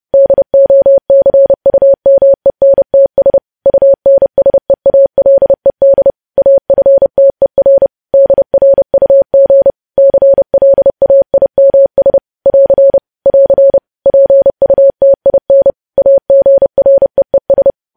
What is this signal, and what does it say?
Morse code, heard as a keyed tone, about 30 wpm, 550 Hz. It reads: DOCUMENTS UNSEALED AFTER DRUG CLAIMS <AR> <AR> PUTIN AGREES
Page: news headlines in morse code at 30 wpm